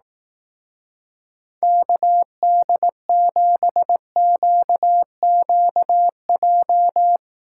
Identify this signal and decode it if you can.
Morse code, heard as a keyed tone, about 18 wpm, 700 Hz. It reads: KD7QQJ
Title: Speed Racing - USA Call Signs 15wpm